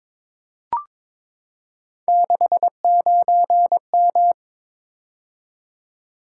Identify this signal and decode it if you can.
Morse code, heard as a keyed tone, about 22 wpm, 700 Hz. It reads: 69M
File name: practice / 22wpm / Three Letter-Number Combinations - 22wpm